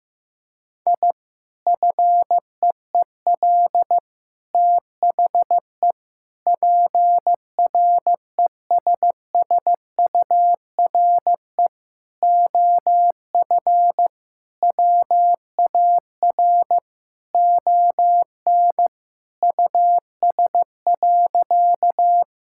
Morse code, heard as a keyed tone, about 15 wpm, 700 Hz. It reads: I FEEL THE PRESSURE OF WAR ON US.
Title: Sentences from Top 1000 Words - Review Part 6 - 15wpm